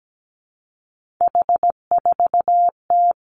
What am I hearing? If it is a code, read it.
Morse code, heard as a keyed tone, about 17 wpm, 700 Hz. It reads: H4T
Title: Three Challenging Letter-Number Combinations 17wpm